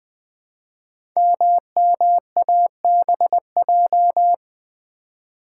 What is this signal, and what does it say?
Morse code, heard as a keyed tone, about 20 wpm, 700 Hz. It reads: MMABJ
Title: Code Groups - 5 Letter:Numbers - 20wpm